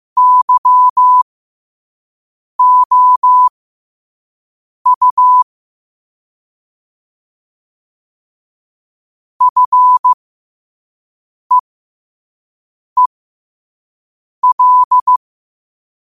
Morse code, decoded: YOU FEEL

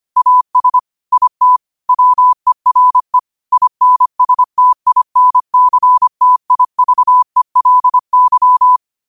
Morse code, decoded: AS IT WERE INSTINCTIVELY

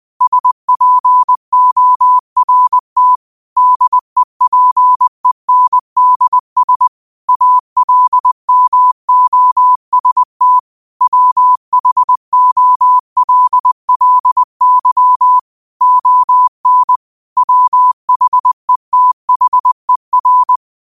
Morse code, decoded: SPORT DEPENDS ALMOST WHOLLY ON WHETHER

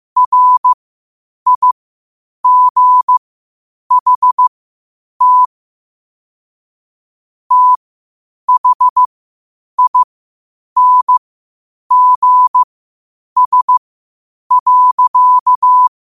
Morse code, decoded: RIGHT THINGS.

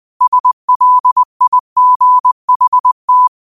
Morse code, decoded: SLIGHT

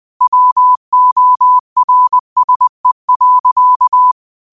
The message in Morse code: WORSE.

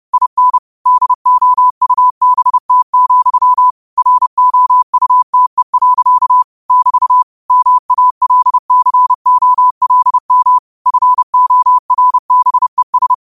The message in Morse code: IN DOUBT, ROUTE. <BT> MALCOLM FORBES

